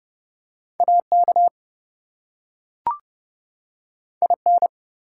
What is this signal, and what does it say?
Morse code, heard as a keyed tone, about 30 wpm, 700 Hz. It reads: AK IN
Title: US States and Canadian Provinces 30wpm